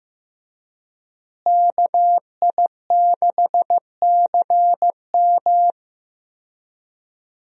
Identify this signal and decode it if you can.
Morse code, heard as a keyed tone, about 16 wpm, 700 Hz. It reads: KI6CM
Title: USA Call Signs - 5 Characters - Repeated 3x - 15wpm